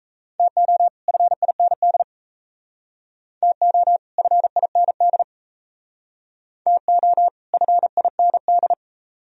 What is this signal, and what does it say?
Morse code, heard as a keyed tone, about 34 wpm, 700 Hz. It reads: TO FIND TO FIND TO FIND
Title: Speed-Racing - Sets of 2 Words 28wpm